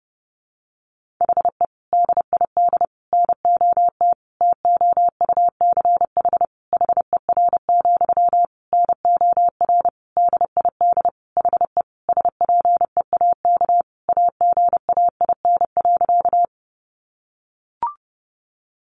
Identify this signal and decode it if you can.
Morse code, heard as a keyed tone, about 30 wpm, 700 Hz. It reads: HE DID NOT TOUCH HER, NOR DID HE SPEAK AGAIN.
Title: Sentences from Top 1500 Words 30wpm